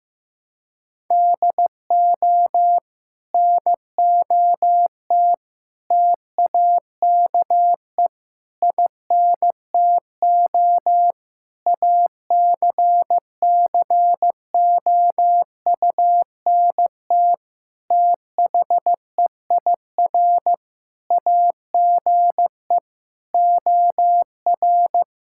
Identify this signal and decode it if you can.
Morse code, heard as a keyed tone, about 15 wpm, 700 Hz. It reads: DO NOT TAKE INTO ACCOUNT THEIR AGE OR
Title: Sentences from Top 1600 Words 15wpm